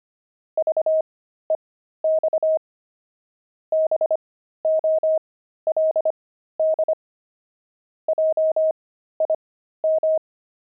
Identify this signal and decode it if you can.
Morse code, heard as a keyed tone, about 25 wpm, spaced as for 13 wpm, 630 Hz. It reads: VEX BOLD JIM
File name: pangram-fs-13-cs-25-zephyrs